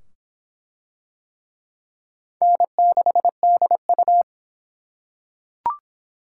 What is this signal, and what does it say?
Morse code, heard as a keyed tone, about 24 wpm, 700 Hz. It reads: N6DU